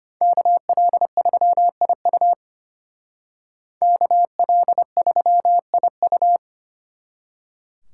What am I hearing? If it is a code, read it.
Morse code, heard as a keyed tone, about 33 wpm, 700 Hz. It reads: KL3IU KL3IU